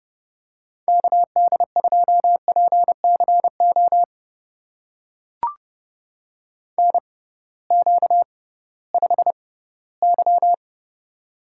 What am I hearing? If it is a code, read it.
Morse code, heard as a keyed tone, about 30 wpm, spaced as for 17 wpm, 700 Hz. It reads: KD2PCO NQ5Y
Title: USA Call Signs - 6x Character Spacing - 30wpm